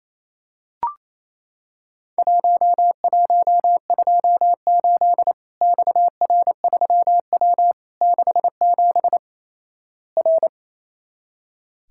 Morse code, heard as a keyed tone, about 28 wpm, 700 Hz. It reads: 1128 XR3W 67 R
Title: Worked All Europe DX Contest - QTC Exchange - 28wpm